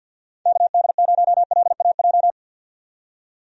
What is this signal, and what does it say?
Morse code, heard as a keyed tone, about 51 wpm, 700 Hz. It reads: KD0LAJ
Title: USA Call Signs - 6 Characters - 50wpm